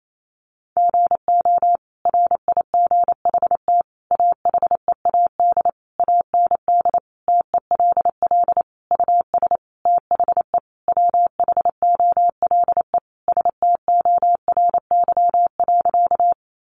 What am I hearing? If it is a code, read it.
Morse code, heard as a keyed tone, about 28 wpm, 700 Hz. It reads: GO RIGHT AHEAD AND TELL US THE WHOLE STORY.
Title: Sentences from Top 1100 Words 28wpm